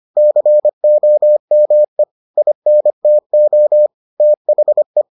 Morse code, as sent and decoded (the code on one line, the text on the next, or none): -.-. --- -- . / .. -. - --- / - .... .
COME INTO THE